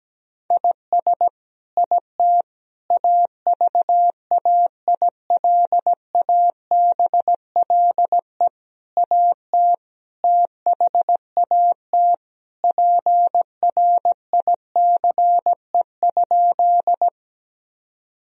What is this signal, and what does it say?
Morse code, heard as a keyed tone, about 17 wpm, 700 Hz. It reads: IS IT AVAILABLE AT THAT PRICE?
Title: Sentences from Top 1000 Words - Review Part 3 - 17wpm